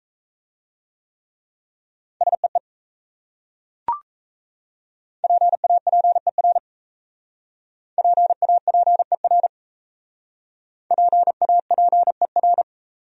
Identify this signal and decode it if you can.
Morse code, heard as a keyed tone, about 38 wpm, 700 Hz. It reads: SEE PAPER PAPER PAPER